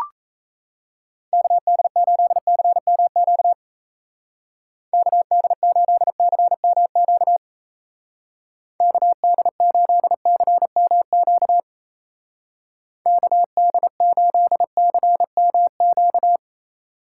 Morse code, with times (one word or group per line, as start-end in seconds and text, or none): 1.3-3.5 KD8CMQ
4.9-7.4 KD8CMQ
8.8-11.6 KD8CMQ
13.1-16.4 KD8CMQ